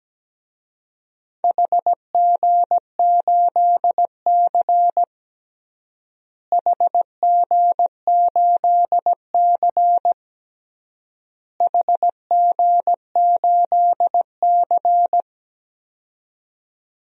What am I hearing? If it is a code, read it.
Morse code, heard as a keyed tone, about 17 wpm, 700 Hz. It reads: HG8C HG8C HG8C
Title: DXCC-100 Call Signs - 4 Characters - Repeated 3x - 17wpm